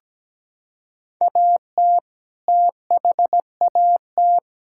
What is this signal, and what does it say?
Morse code, heard as a keyed tone, about 17 wpm, 700 Hz. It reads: AT THAT
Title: Sentences from Top 1000 Words - Review Part 5 - 17wpm